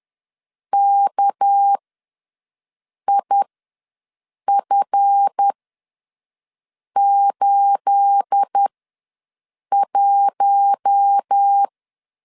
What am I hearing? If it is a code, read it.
Morse code, heard as a keyed tone, about 10 wpm, spaced as for 5 wpm, 800 Hz. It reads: KIF81
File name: LEZIONI / 9 2867 ws 10wpm 4215